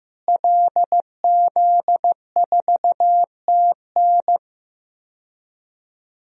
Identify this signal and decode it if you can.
Morse code, heard as a keyed tone, about 15 wpm, 700 Hz. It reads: LZ4TN
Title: DXCC-100 Call Signs - 5 Characters - Repeated 3x - 15wpm